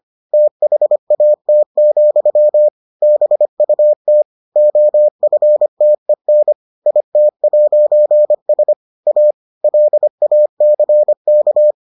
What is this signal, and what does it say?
Morse code, heard as a keyed tone, about 25 wpm, 600 Hz. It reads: THAT, BUT OFTEN IT'S A LACK